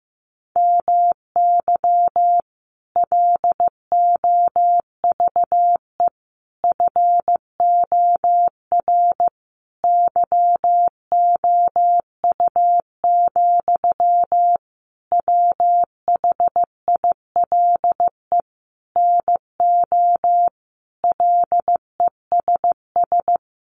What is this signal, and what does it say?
Morse code, heard as a keyed tone, about 15 wpm, 700 Hz. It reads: MY LOVE FOR YOU, WHILE NO LESS
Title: Sentences from Top 1400 Words 15wpm